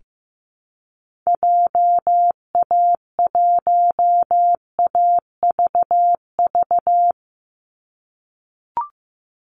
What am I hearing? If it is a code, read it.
Morse code, heard as a keyed tone, about 15 wpm, 700 Hz. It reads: JA1AVV
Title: USA Call Signs 15wpm